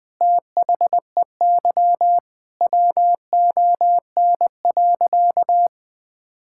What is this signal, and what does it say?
Morse code, heard as a keyed tone, about 21 wpm, 700 Hz. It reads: THEY WON.
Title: Sentences with 2-Word Phrases - Spoken Beginning - 20wpm